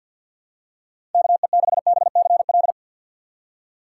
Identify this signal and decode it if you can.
Morse code, heard as a keyed tone, about 51 wpm, 700 Hz. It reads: KE6BCL
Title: USA Call Signs - 6 Characters - 3x Character Spacing - 50wpm